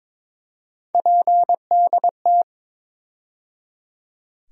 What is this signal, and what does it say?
Morse code, heard as a keyed tone, about 23 wpm, 700 Hz. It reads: PDT
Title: USA Call Sign Postfixes - 3 Letters 22wpm